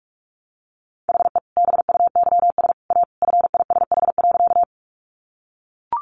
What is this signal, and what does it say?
Morse code, heard as a keyed tone, about 44 wpm, 700 Hz. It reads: HE BUYS A FISH.
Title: Easy Phrases 45wpm